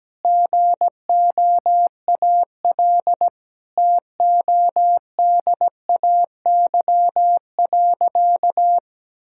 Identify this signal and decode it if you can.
Morse code, heard as a keyed tone, about 17 wpm, 700 Hz. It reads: GOAL TODAY.
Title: Sentences from Top 1000 Words - Review Part 5 - 17wpm